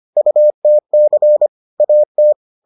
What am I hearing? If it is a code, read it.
Morse code, heard as a keyed tone, about 25 wpm, 600 Hz. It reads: UTC AT